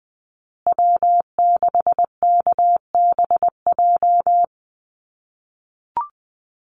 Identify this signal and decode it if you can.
Morse code, heard as a keyed tone, about 20 wpm, 700 Hz. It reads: W6KBJ
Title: USA Call Signs Repeated 3x 20wpm